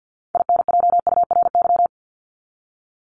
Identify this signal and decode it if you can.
Morse code, heard as a keyed tone, about 50 wpm, 700 Hz. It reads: INJURY
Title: Top 1751-2000 Words 50wpm